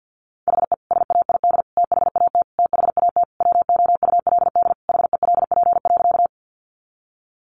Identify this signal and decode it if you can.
Morse code, heard as a keyed tone, about 48 wpm, 700 Hz. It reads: HE SAID THAT THAT WOULD HELP.
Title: Sentences from Top 200 Words 50wpm